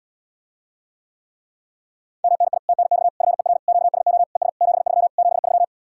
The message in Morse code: MNI TNX FER QSO ES 73 73

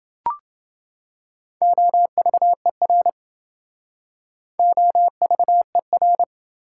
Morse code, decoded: OVER OVER

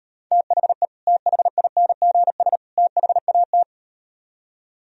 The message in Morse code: THE THINGS THAT

